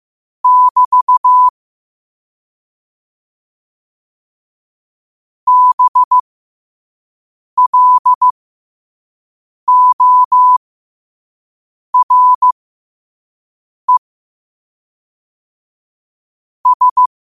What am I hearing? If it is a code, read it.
Morse code, heard as a keyed tone, about 15 wpm, spaced as for 5 wpm, 1000 Hz. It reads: <BT> BLORE S
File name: qotdfiles / 5wpm